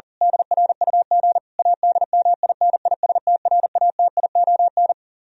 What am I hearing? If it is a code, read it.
Morse code, heard as a keyed tone, about 37 wpm, 700 Hz. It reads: DRUG ADMINISTRATION
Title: Speed-Racing - Sets of 5 Words 40wpm